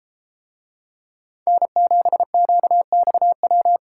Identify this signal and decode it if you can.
Morse code, heard as a keyed tone, about 32 wpm, 700 Hz. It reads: N7QXW